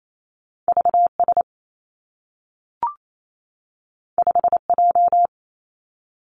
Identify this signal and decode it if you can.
Morse code, heard as a keyed tone, about 28 wpm, 700 Hz. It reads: V S 5 J
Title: Two Challenging Letter-Number Combinations 28wpm